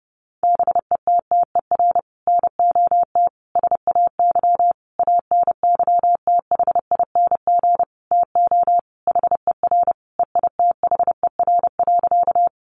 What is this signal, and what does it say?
Morse code, heard as a keyed tone, about 30 wpm, 700 Hz. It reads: BETTER NOT SAY ANYTHING TO HER EITHER.
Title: Sentences from Top 1000 Words 30wpm